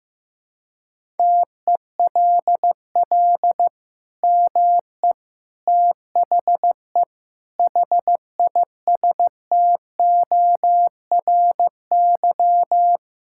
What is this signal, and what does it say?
Morse code, heard as a keyed tone, about 16 wpm, 700 Hz. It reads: TELL ME THE HISTORY
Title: Sentences from Top 1000 Words - Review Part 5 - 15wpm